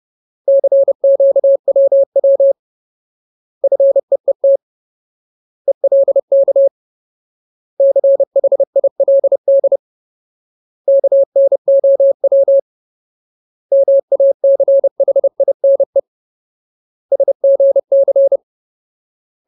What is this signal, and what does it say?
Morse code, heard as a keyed tone, about 30 wpm, 550 Hz. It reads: CQWW FEET ELK CHILD KNOW MACHINE SGC
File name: Ultimate Word List / 30WPM Ultimate